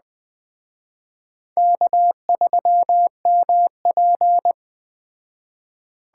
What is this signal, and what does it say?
Morse code, heard as a keyed tone, about 20 wpm, 700 Hz. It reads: K3MP
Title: Speed Racing - USA 1x2 Call Signs 17wpm